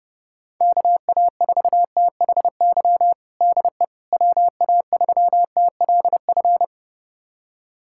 Morse code, heard as a keyed tone, about 30 wpm, 700 Hz. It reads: KA4THY DE WA3TLF
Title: USA Callsign DE Callsign 30wpm